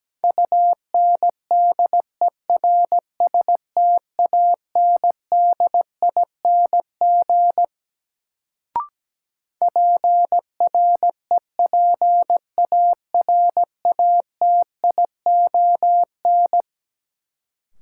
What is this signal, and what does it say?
Morse code, heard as a keyed tone, about 17 wpm, 700 Hz. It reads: UNDERSTANDING PREPARATION
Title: Sentences from Top 1400 Words - New Word Drill Alt - 17wpm